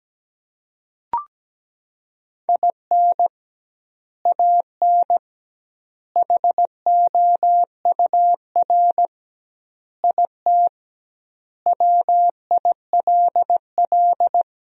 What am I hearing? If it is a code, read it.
Morse code, heard as a keyed tone, about 17 wpm, 700 Hz. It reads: IN AN HOUR IT WILL
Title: 2x Word-Spacing - Sentences from Top 400 Words - 17wpm